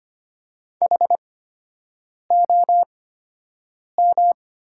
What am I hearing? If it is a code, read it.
Morse code, heard as a keyed tone, about 25 wpm, spaced as for 7 wpm, 700 Hz. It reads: HOM
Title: 8x Character Spacing - Top 300 Words - 25wpm